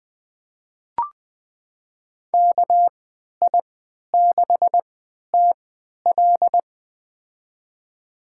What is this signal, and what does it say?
Morse code, heard as a keyed tone, about 21 wpm, spaced as for 11 wpm, 700 Hz. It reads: KI6TL
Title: USA Call Signs - 5 Characters - 3x Character Spacing - 20wpm